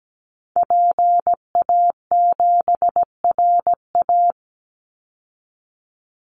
Morse code, heard as a keyed tone, about 17 wpm, 700 Hz. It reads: PA7RA
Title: Call Signs Repeated 3x - 17wpm